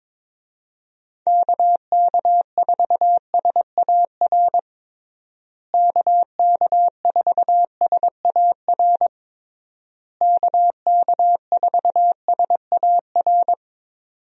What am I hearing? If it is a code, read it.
Morse code, heard as a keyed tone, about 22 wpm, 700 Hz. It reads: KK4SAR KK4SAR KK4SAR
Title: USA Call Signs - 6 Characters - Repeated 3x - 22wpm